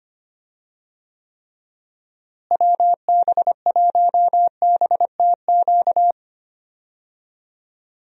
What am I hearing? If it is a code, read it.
Morse code, heard as a keyed tone, about 25 wpm, 700 Hz. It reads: WB1BTQ
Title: USA Call Signs - 6 Characters - Repeated 3x - 25wpm